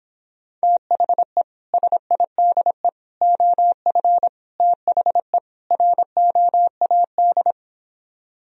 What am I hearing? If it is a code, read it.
Morse code, heard as a keyed tone, about 26 wpm, 700 Hz. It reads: THE SIDE OF THE ROAD